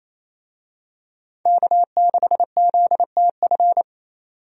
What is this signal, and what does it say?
Morse code, heard as a keyed tone, about 27 wpm, 700 Hz. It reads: K6ZTF
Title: USA Call Signs - 5 Characters - 28wpm